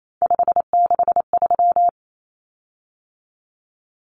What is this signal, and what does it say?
Morse code, heard as a keyed tone, about 28 wpm, 700 Hz. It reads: 563
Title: Common Cut Numbers - 3-Digit 28wpm